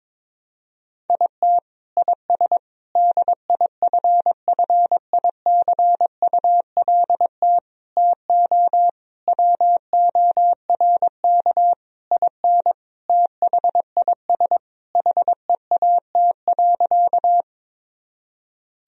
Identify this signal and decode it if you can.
Morse code, heard as a keyed tone, about 21 wpm, 700 Hz. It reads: IT IS DIFFICULT TO WORK IN THIS HEAT.
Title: Sentences from Top 1000 Words - Review Part 3 - 22wpm